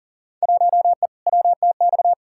Morse code, heard as a keyed tone, about 40 wpm, 700 Hz. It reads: 1E WTX
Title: Field Day Exchanges - 40wpm